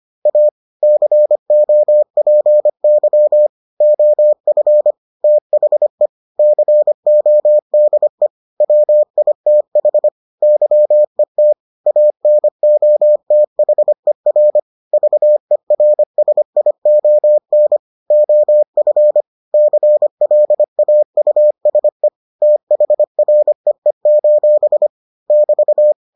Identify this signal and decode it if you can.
Morse code, heard as a keyed tone, about 25 wpm, 600 Hz. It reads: A COPY OF THE CODE WITH YET ANOTHER VERSION OF CLAUSE THREE: <BT>